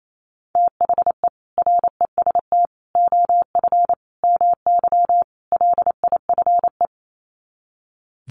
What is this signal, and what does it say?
Morse code, heard as a keyed tone, about 28 wpm, 700 Hz. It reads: THE REST OF MY LIFE
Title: Sets of 5 Words 28wpm